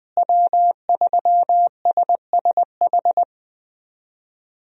Morse code, heard as a keyed tone, about 20 wpm, 700 Hz. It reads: W3SSH
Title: USA Call Signs - 5 Characters - Repeated 2x - 20wpm